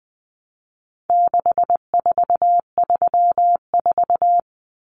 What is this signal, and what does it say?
Morse code, heard as a keyed tone, about 20 wpm, 700 Hz. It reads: 6434
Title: Numbers - 4-Digit 20wpm